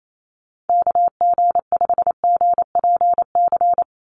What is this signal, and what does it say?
Morse code, heard as a keyed tone, about 28 wpm, 700 Hz. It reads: KG5GPC
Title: USA Call Signs 28wpm